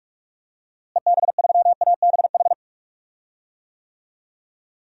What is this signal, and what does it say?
Morse code, heard as a keyed tone, about 46 wpm, 700 Hz. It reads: EB3ABH
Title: DXCC-100 Call Signs - 6 Characters - Repeated 3x - 45wpm